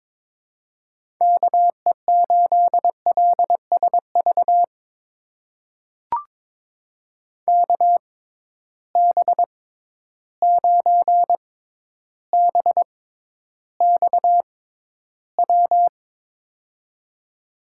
Morse code, heard as a keyed tone, about 22 wpm, spaced as for 11 wpm, 700 Hz. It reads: KE8LSV KB9BXW
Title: USA Call Signs - 6 Characters - 6x Character Spacing - 22wpm